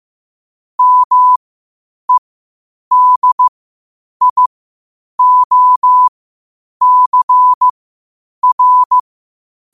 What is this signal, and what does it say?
Morse code, heard as a keyed tone, about 15 wpm, spaced as for 8 wpm, 1000 Hz. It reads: MEDIOCR
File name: qotdfiles / 8wpm